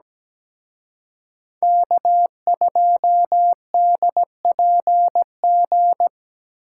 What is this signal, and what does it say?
Morse code, heard as a keyed tone, about 20 wpm, 700 Hz. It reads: K2DPG
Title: Speed Racing - USA Call Signs 17wpm